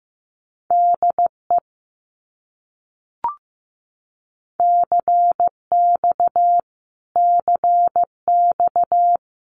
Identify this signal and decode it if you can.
Morse code, heard as a keyed tone, about 15 wpm, 700 Hz. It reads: DE CX CX
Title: Abbreviations Repeated 3x - 15wpm